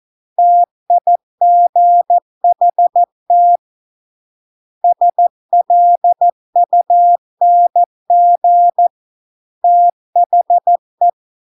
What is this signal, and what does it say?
Morse code, heard as a keyed tone, about 14 wpm, 700 Hz. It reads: TIGHT SLUNG THE